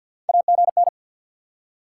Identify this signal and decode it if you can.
Morse code, heard as a keyed tone, about 48 wpm, 690 Hz. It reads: AGN
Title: SOTA QSO Practice 50wpm